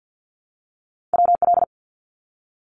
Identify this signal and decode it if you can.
Morse code, heard as a keyed tone, about 50 wpm, 700 Hz. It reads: WL
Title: Abbreviations Repeated 3x - 50wpm